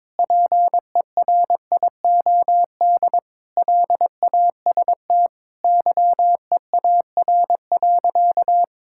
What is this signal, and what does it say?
Morse code, heard as a keyed tone, about 22 wpm, 700 Hz. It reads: PERIOD LAST YEAR.
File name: Sentences with 7-Word Phrases - 22wpm